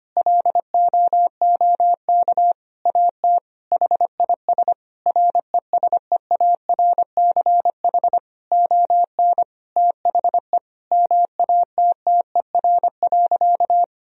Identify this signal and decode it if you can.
Morse code, heard as a keyed tone, about 25 wpm, 700 Hz. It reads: LOOK AT HIS RESEARCH ON THE MATTER.
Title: Sentences from Top 1000 Words - Review Part 4 - 25wpm